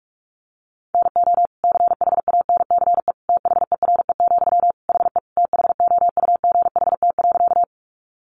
Morse code, heard as a keyed tone, about 44 wpm, 700 Hz. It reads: NO CHANCE THERE, HE THOUGHT.
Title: Sentences from Top 800 Words 45wpm